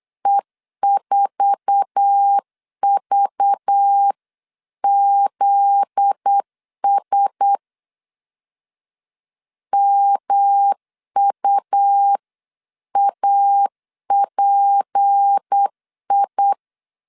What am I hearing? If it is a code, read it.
Morse code, heard as a keyed tone, about 8 wpm, 800 Hz. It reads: E4VZS MUAPI
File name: CW / DET / 1 9 1500C S C 8wpm 5653